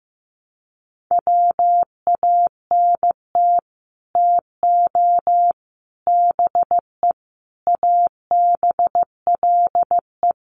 Morse code, heard as a keyed tone, about 15 wpm, 700 Hz. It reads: WANT TO BE ABLE
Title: Sets of 5 Words 15wpm